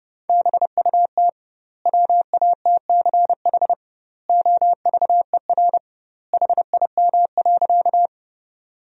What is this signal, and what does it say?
Morse code, heard as a keyed tone, about 30 wpm, 700 Hz. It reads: BUT WATCH OVER HIM.
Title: 2x Word-Spacing - Sentences from Top 300 Words - 30wpm